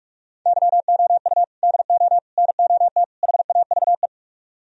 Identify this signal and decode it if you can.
Morse code, heard as a keyed tone, about 40 wpm, 700 Hz. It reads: YOU DO NOT HAVE